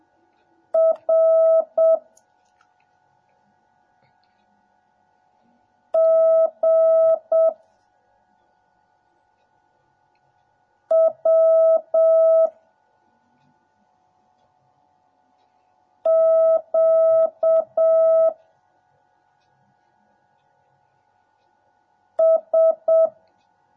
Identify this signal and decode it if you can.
Morse code, heard as a keyed tone, about 7 wpm, 650 Hz. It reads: R G W Q S